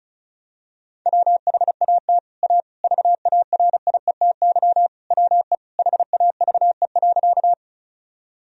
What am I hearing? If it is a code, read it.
Morse code, heard as a keyed tone, about 34 wpm, 700 Hz. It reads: WHAT A VARIETY WE HAVE.
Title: Sentences from Top 1000 Words - Review Part 4 - 35wpm